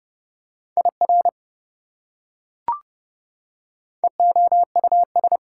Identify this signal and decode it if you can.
Morse code, heard as a keyed tone, about 30 wpm, 700 Hz. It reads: IR EOUS
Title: Prefixes and Suffixes 30wpm